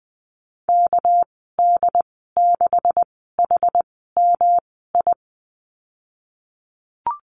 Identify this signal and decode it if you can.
Morse code, heard as a keyed tone, about 19 wpm, spaced as for 14 wpm, 700 Hz. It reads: KD6HMI
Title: 231 Lesson 37g - Letter x - Call Signs 20wpm